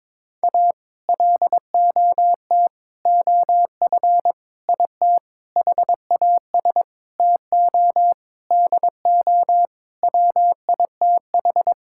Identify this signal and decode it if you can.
Morse code, heard as a keyed tone, about 22 wpm, 700 Hz. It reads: A LOT OF IT HAS TO DO WITH
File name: Sets of 9 Words - 22wpm